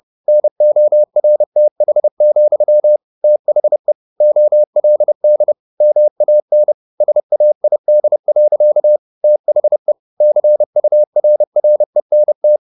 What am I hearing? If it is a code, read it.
Morse code, heard as a keyed tone, about 30 wpm, 600 Hz. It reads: NORTH, THE OLD MAN SAID. THE CURRENT
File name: the old man and the sea / the old man and the sea 1 30wpm